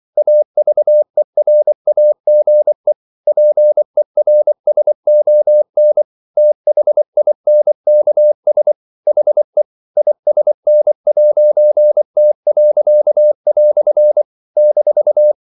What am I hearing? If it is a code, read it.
Morse code, heard as a keyed tone, about 24 wpm, 600 Hz. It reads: AVERAGE PERSON THINKS HE ISN'T." -
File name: CW / Practice / EnglishQuotations0424